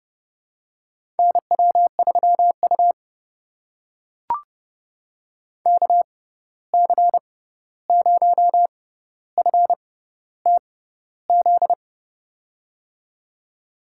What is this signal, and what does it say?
Morse code, heard as a keyed tone, about 30 wpm, spaced as for 14 wpm, 700 Hz. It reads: NW3U KC0FTZ
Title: USA Call Signs - 6x Character Spacing - 30wpm